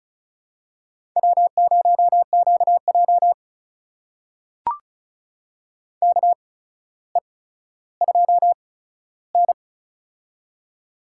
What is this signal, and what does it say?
Morse code, heard as a keyed tone, about 36 wpm, spaced as for 15 wpm, 700 Hz. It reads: W0QJ KE2N
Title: USA Call Signs - 4 Characters - 8x Character Spacing - 35wpm